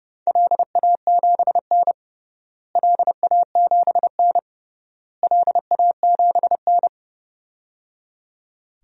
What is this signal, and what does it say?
Morse code, heard as a keyed tone, about 30 wpm, 700 Hz. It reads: LA7N LA7N LA7N
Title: DXCC-100 Call Signs - 4 Characters - Repeated 3x - 30wpm